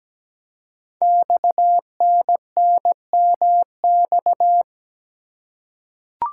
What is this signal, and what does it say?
Morse code, heard as a keyed tone, about 17 wpm, 700 Hz. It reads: XNNMX
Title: Code Groups - 5 Letters - 17wpm